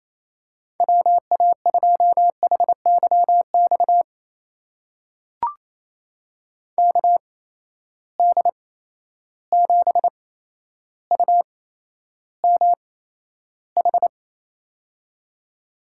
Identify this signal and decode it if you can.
Morse code, heard as a keyed tone, about 28 wpm, spaced as for 12 wpm, 700 Hz. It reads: WA2HYX KD7UMH
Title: USA Call Signs - 6 Characters - 8x Character Spacing - 28wpm